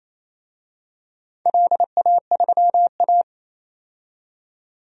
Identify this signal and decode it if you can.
Morse code, heard as a keyed tone, about 28 wpm, 700 Hz. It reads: LA3A
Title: DXCC-100 Call Signs - 4 Characters - 28wpm